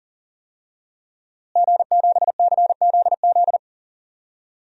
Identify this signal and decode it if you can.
Morse code, heard as a keyed tone, about 40 wpm, 700 Hz. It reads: G7CZZ
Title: DXCC-100 Call Signs - 5 Characters - Repeated 2x - 40wpm